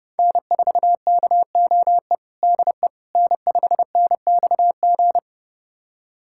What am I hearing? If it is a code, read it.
Morse code, heard as a keyed tone, about 30 wpm, 700 Hz. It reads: N4KOE DE N5NXG
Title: USA Callsign DE Callsign - 5 Characters - 30wpm